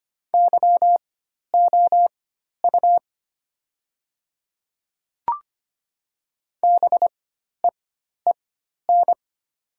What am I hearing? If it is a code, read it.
Morse code, heard as a keyed tone, about 25 wpm, spaced as for 12 wpm, 700 Hz. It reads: YOU BEEN
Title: 4x Character Spacing - Top 200 Words - 25wpm